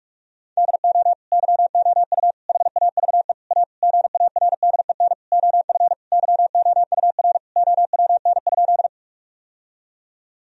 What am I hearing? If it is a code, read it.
Morse code, heard as a keyed tone, about 44 wpm, 700 Hz. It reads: DO YOU HAVE A GARDEN OF YOUR OWN?
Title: Sentences from Top 1000 Words - Review Part 4 - 45wpm